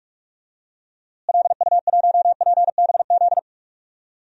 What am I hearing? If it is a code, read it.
Morse code, heard as a keyed tone, about 41 wpm, 700 Hz. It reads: LU1PBZ
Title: DXCC-100 Call Signs - 6 Characters - Repeated 2x - 45wpm